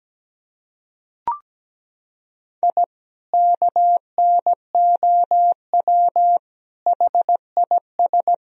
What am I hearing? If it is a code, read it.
Morse code, heard as a keyed tone, about 17 wpm, 700 Hz. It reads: I KNOW HIS
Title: Sentences from Top 1000 Words - Review Part 3 - 17wpm